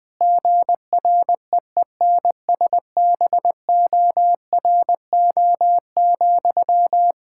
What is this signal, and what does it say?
Morse code, heard as a keyed tone, about 20 wpm, 700 Hz. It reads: GREENSBORO,
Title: K1USN SST QSO Practice 20wpm